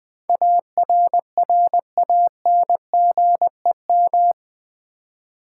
Morse code, heard as a keyed tone, about 20 wpm, 700 Hz. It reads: ARRANGEM
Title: Build A Word - 4 Letters At A Time - 11-Letter Words - 20wpm